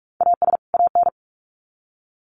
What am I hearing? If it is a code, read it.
Morse code, heard as a keyed tone, about 44 wpm, 700 Hz. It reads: AS AN
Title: Sets of 2 Words 45wpm